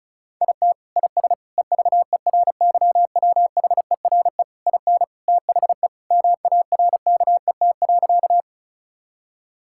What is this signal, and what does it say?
Morse code, heard as a keyed tone, about 36 wpm, 700 Hz. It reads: IT IS EVERYWHERE IN THE MARKET.
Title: Sentences from Top 1600 Words 35wpm